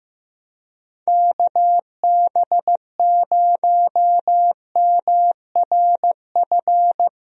Morse code, heard as a keyed tone, about 15 wpm, 700 Hz. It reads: KB0MRF
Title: USA Call Signs - 6 Characters - 15wpm